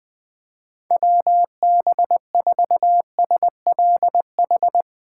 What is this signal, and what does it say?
Morse code, heard as a keyed tone, about 20 wpm, 700 Hz. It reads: WB4SLH